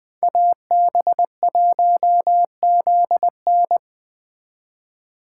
Morse code, heard as a keyed tone, about 20 wpm, 700 Hz. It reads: AB1ZN
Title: Speed Racing - USA 2x2 Call Signs 17wpm